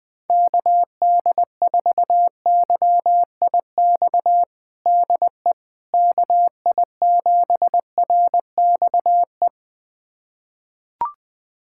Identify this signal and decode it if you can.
Morse code, heard as a keyed tone, about 21 wpm, 700 Hz. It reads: KD4YIX DE KI7RXE
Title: USA Callsign DE Callsign - 6 Characters - 20wpm